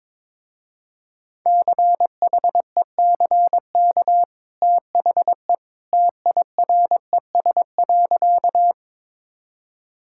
Morse code, heard as a keyed tone, about 22 wpm, 700 Hz. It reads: CHECK THE TIRES.
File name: practice / 22wpm / Easy Sentences with Hints - 22wpm